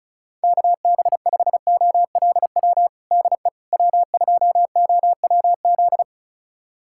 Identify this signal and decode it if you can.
Morse code, heard as a keyed tone, about 36 wpm, 700 Hz. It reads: KB5OLW DE W2OWZ
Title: USA Callsign DE Callsign 35wpm